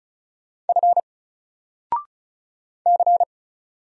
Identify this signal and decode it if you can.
Morse code, heard as a keyed tone, about 36 wpm, 700 Hz. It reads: F C
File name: Single-Letter-Number-35wpm